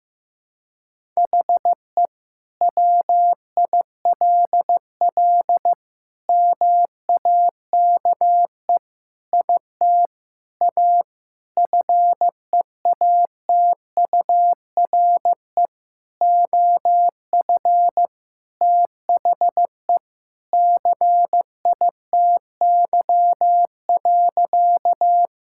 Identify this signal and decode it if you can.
Morse code, heard as a keyed tone, about 15 wpm, 700 Hz. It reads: HE WILL MAKE IT A FEATURE OF THE CITY.
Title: Sentences from Top 1000 Words - Review Part 5 - 15wpm